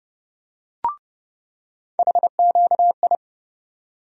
Morse code, heard as a keyed tone, about 30 wpm, 700 Hz. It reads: HQI
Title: USA Call Sign Postfixes - 3 Letters 30wpm